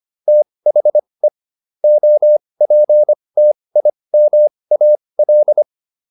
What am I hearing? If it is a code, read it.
Morse code, heard as a keyed tone, about 25 wpm, spaced as for 20 wpm, 600 Hz. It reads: THE OPTIMAL